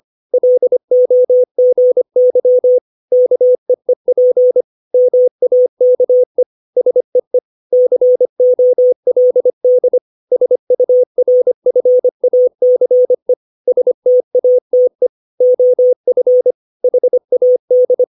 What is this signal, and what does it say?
Morse code, heard as a keyed tone, about 25 wpm, 500 Hz. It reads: LOGY KEEP MAKE SEE COLD SURFACE STATE OF HAD
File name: 632 words / 632 Words 25WPM